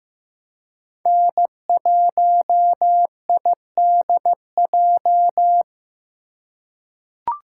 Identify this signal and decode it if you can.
Morse code, heard as a keyed tone, about 15 wpm, 700 Hz. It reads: N1IDJ
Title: USA Call Signs - 3x Character Spacing - 15wpm